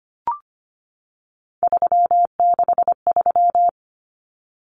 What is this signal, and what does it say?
Morse code, heard as a keyed tone, about 25 wpm, 700 Hz. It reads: 363
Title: Common Cut Numbers - 3-Digit 25wpm